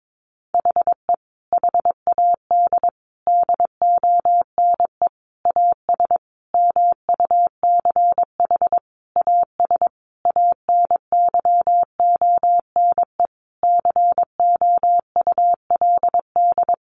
Morse code, heard as a keyed tone, about 22 wpm, 700 Hz. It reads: HE HAD DONE AS MUCH AS ANYONE COULD
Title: Sentences from Top 1200 Words 22wpm